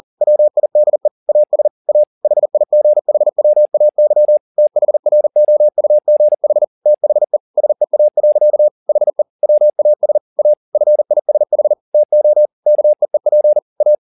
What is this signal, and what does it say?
Morse code, heard as a keyed tone, about 40 wpm, 600 Hz. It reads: WIDE AS A HIGHWAY THROUGH THE SEA. HE WAS A FISH TO KEEP A
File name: the old man and the sea 2 40wpm